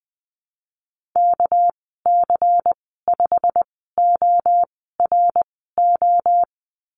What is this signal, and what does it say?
Morse code, heard as a keyed tone, about 19 wpm, spaced as for 14 wpm, 700 Hz. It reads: KC5ORO
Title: 219 Lesson 35g - Number 0 - Call Signs 20wpm